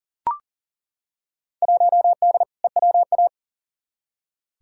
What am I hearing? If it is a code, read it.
Morse code, heard as a keyed tone, about 38 wpm, 700 Hz. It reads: 1D EWA
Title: Field Day Exchanges - 40wpm